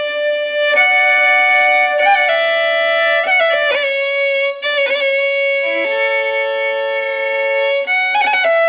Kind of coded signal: dial-up modem handshake